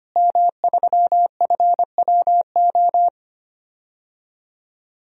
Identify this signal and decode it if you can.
Morse code, heard as a keyed tone, about 25 wpm, 700 Hz. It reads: M3FWO
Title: DXCC-100 Call Signs - 5 Characters - Repeated 3x - 25wpm